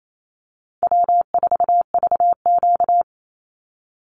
Morse code, heard as a keyed tone, about 28 wpm, 700 Hz. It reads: W4VQ
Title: Easy Call Signs Repeated 3x - 28wpm